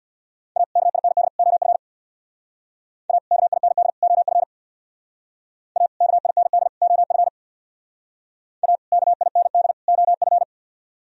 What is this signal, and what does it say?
Morse code, heard as a keyed tone, about 63 wpm, 700 Hz. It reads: A KIND OF A KIND OF A KIND OF A KIND OF